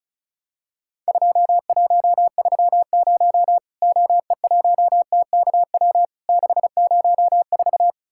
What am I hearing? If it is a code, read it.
Morse code, heard as a keyed tone, about 36 wpm, 700 Hz. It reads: 2130 OE1TKW 604
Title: Worked All Europe DX Contest - QTC Exchange - 35wpm